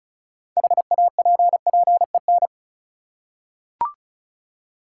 Morse code, heard as a keyed tone, about 34 wpm, 700 Hz. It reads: HAPPEN
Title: Six Letter Words - 35wpm